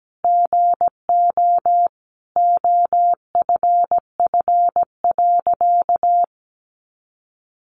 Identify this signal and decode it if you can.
Morse code, heard as a keyed tone, about 17 wpm, 700 Hz. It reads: GO OFF.
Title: Sentences from Top 600 Words 17wpm